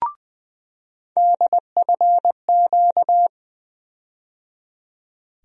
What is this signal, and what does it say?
Morse code, heard as a keyed tone, about 20 wpm, 700 Hz. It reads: DFQ
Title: USA Call Sign Postfixes - 3 Letters 20wpm